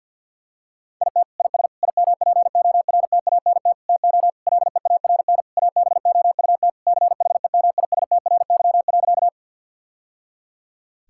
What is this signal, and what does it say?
Morse code, heard as a keyed tone, about 50 wpm, 700 Hz. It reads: IT IS IMPORTANT TO LEARN ABOUT CHEMISTRY.
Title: Sentences from Top 1000 Words - Review Part 6 - 50wpm